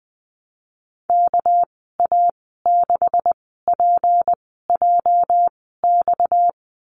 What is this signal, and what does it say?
Morse code, heard as a keyed tone, about 20 wpm, spaced as for 15 wpm, 700 Hz. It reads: KA6PJX
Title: 238 Lesson 38g - Letter q - Call Signs 20wpm